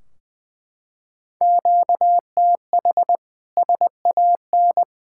Morse code, heard as a keyed tone, about 20 wpm, 700 Hz. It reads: QTH SAN
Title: Practice QSO - Exchange 1 - Beginning - 20wpm